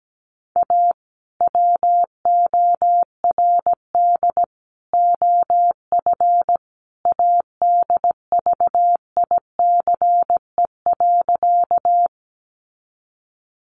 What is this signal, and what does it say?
Morse code, heard as a keyed tone, about 17 wpm, 700 Hz. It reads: A WORD OF ADVICE.
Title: Sentences from Top 1100 Words 17wpm